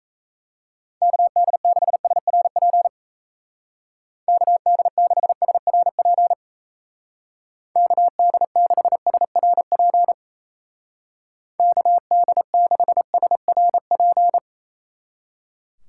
Morse code, heard as a keyed tone, about 34 wpm, 700 Hz. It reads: KD6SRP KD6SRP KD6SRP KD6SRP
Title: Speed Racing - USA 2x3 Call Signs 28wpm